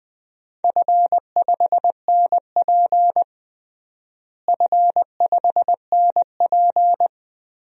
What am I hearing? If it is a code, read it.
Morse code, heard as a keyed tone, about 20 wpm, 700 Hz. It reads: F5NP F5NP
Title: DXCC-100 Call Signs - 4 Characters - Repeated 3x - 20wpm